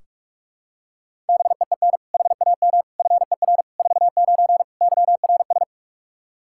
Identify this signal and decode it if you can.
Morse code, heard as a keyed tone, about 46 wpm, 700 Hz. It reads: BEEN HAM FER 49 YRS
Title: Practice QSO - Exchange 3 - More Info - 45wpm